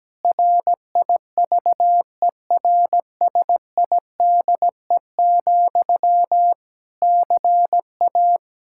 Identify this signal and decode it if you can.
Morse code, heard as a keyed tone, about 17 wpm, 700 Hz. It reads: RIVERSIDE, CA
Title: Practice QSO - Exchange 1 - Beginning - 17wpm